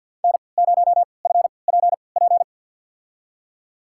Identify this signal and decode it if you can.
Morse code, heard as a keyed tone, about 52 wpm, spaced as for 29 wpm, 700 Hz. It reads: N0FPP
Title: USA Call Signs - 3x Character Spacing - 50wpm